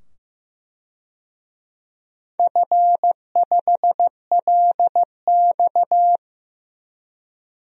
Morse code, heard as a keyed tone, about 15 wpm, 700 Hz. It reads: F5LX
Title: DXCC-100 Call Signs - 4 Characters - 15wpm